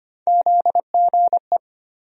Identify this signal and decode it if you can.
Morse code, heard as a keyed tone, about 25 wpm, 700 Hz. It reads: ZGE
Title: USA Call Sign Postfixes - 3 Letters 25wpm